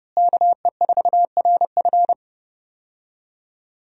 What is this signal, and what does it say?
Morse code, heard as a keyed tone, about 31 wpm, 700 Hz. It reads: KE4RF
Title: USA Call Signs - 5 Characters - Repeated 3x - 30wpm